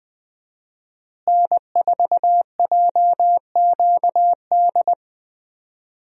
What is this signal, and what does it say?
Morse code, heard as a keyed tone, about 21 wpm, 700 Hz. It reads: N4JQD
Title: USA Call Signs - 5 Characters - Repeated 3x - 20wpm